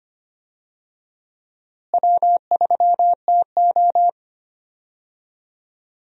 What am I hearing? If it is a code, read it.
Morse code, heard as a keyed tone, about 25 wpm, 700 Hz. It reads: W3TO
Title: USA Call Signs - 4 Characters - Repeated 3x - 25wpm